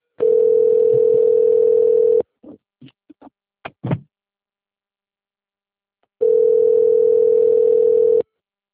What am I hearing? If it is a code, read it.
Ringback tone (North America), 440 + 480 Hz.
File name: rg-1600-unknown- - - .9640